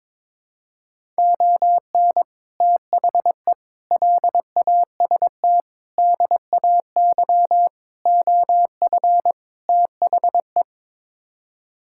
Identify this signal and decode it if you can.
Morse code, heard as a keyed tone, about 22 wpm, 700 Hz. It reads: ON THE LAST DAY OF THE
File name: practice / 22wpm / Sets of 6 Words - 22wpm